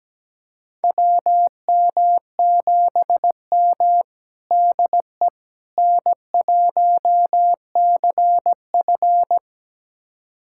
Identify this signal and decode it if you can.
Morse code, heard as a keyed tone, about 17 wpm, 700 Hz. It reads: WM7M DE N1CF
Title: USA Callsign DE Callsign - 4 Characters - 17wpm